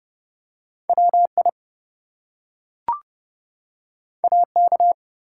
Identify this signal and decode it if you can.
Morse code, heard as a keyed tone, about 30 wpm, 700 Hz. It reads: WI AK
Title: US States and Canadian Provinces 30wpm